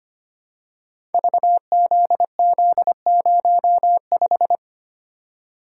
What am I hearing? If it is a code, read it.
Morse code, heard as a keyed tone, about 25 wpm, 700 Hz. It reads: VZZ05
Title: Code Groups - 5 Letter:Numbers - 25wpm